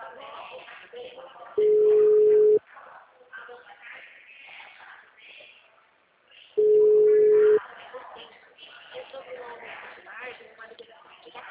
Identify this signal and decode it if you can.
Ringback tone (Europe, 425 Hz), with speech.